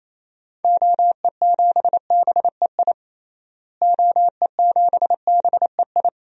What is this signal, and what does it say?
Morse code, heard as a keyed tone, about 27 wpm, 700 Hz. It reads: OE7BEI OE7BEI
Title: DXCC-100 Call Signs - 6 Characters - Repeated 2x - 28wpm